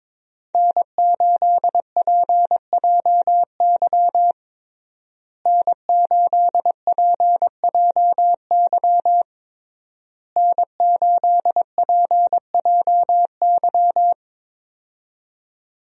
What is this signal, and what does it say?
Morse code, heard as a keyed tone, about 22 wpm, 700 Hz. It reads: N8PJY N8PJY N8PJY
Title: USA Call Signs - 5 Characters - Repeated 3x - 22wpm